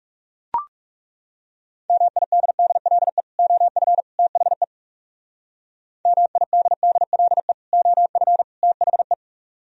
Morse code, heard as a keyed tone, about 36 wpm, 700 Hz. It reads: MIDDLE OF THE MIDDLE OF THE
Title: Speed-Racing - Sets of 3 Words 30wpm